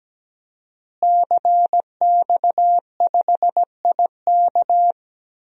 Morse code, heard as a keyed tone, about 17 wpm, 700 Hz. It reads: CX5IK
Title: DXCC-100 Call Signs - 5 Characters - Repeated 2x - 17wpm